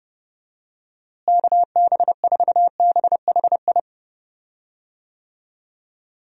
Morse code, heard as a keyed tone, about 30 wpm, 700 Hz. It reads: KB4BHI
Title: USA Call Signs - 6 Characters - Repeated 2x - 30wpm